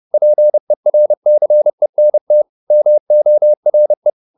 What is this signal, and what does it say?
Morse code, heard as a keyed tone, about 30 wpm, 600 Hz. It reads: PERCENT MORE